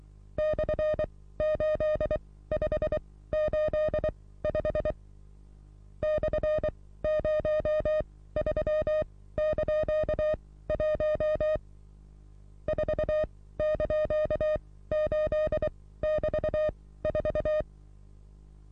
Morse code, heard as a keyed tone, about 23 wpm, spaced as for 16 wpm, 610 Hz. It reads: /8585 /03)1 4)8<BT>4
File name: repos / Morsekurs / DJ4UF Morsen27